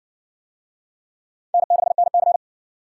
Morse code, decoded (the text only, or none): N6NC